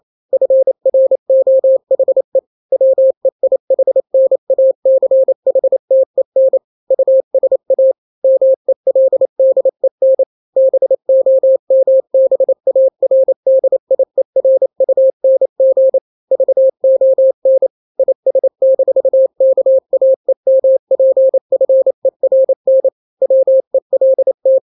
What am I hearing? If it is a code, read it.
Morse code, heard as a keyed tone, about 28 wpm, 530 Hz. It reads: FROHE WEIHNACHTEN USA MELDEN BOMBARDIERUNG VON IS-KAEMPFERN WELT
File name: daily / 27wpm